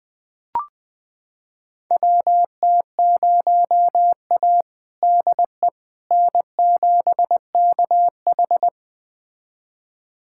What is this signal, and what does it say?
Morse code, heard as a keyed tone, about 20 wpm, 700 Hz. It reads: WT0A DE N7KH
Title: USA Callsign DE Callsign - 4 Characters - 20wpm